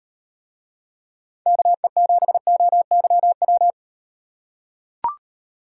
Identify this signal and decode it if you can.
Morse code, heard as a keyed tone, about 36 wpm, 700 Hz. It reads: KE7OYW